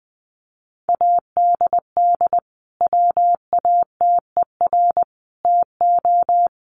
Morse code, heard as a keyed tone, about 20 wpm, 700 Hz. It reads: ADD WATER TO
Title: Sentences from Top 1500 Words 20wpm